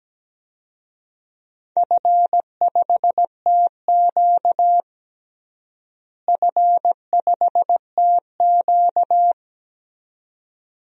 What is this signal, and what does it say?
Morse code, heard as a keyed tone, about 17 wpm, 700 Hz. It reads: F5TQ F5TQ
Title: DXCC-100 Call Signs - 4 Characters - Repeated 2x - 17wpm